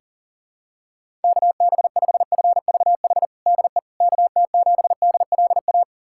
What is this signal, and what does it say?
Morse code, heard as a keyed tone, about 37 wpm, 700 Hz. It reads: KB5FVH DE KT7DLA
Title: USA Callsign DE Callsign 40wpm